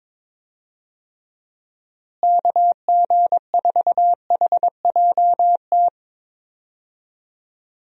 Morse code, heard as a keyed tone, about 22 wpm, 700 Hz. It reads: KG4HJT
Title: USA Call Signs - 6 Characters - Repeated 3x - 22wpm